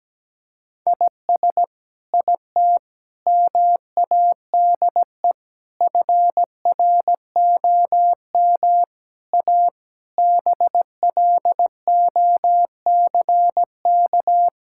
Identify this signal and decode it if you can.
Morse code, heard as a keyed tone, about 17 wpm, 700 Hz. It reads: IS IT MADE FROM A BLOCK
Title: Sentences from Top 1600 Words 17wpm